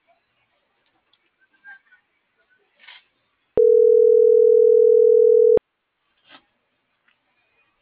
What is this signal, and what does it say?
Ringback tone (North America), 440 + 480 Hz.